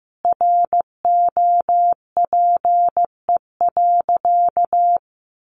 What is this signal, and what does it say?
Morse code, heard as a keyed tone, about 15 wpm, 700 Hz. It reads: ROPE.
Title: Easy Phrases 15wpm